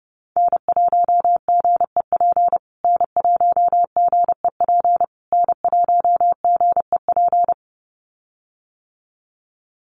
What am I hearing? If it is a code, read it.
Morse code, heard as a keyed tone, about 30 wpm, 700 Hz. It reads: N1GEP N1GEP N1GEP
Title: USA Call Signs Repeated 3x 30wpm